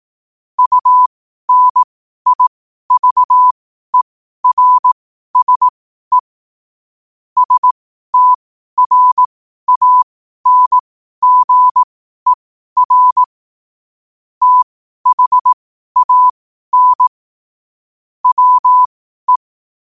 Morse code, heard as a keyed tone, about 18 wpm, spaced as for 12 wpm, 1000 Hz. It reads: UNIVERSE STRANGER THAN WE
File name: qotdfiles / 12wpm